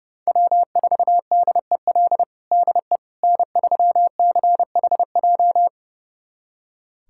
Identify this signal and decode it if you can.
Morse code, heard as a keyed tone, about 30 wpm, 700 Hz. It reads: W4DEL DE N3CHJ
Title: USA Callsign DE Callsign - 5 Characters - 30wpm